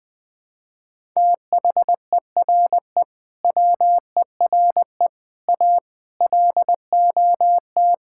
Morse code, heard as a keyed tone, about 20 wpm, 700 Hz. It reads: THERE WERE A LOT
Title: Sentences with 5-Word Phrases - Spoken Beginning - 20wpm